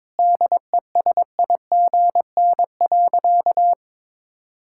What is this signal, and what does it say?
Morse code, heard as a keyed tone, about 22 wpm, 700 Hz. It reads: DESIGN.
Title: Sentences from Top 1000 Words - Review Part 5 - 22wpm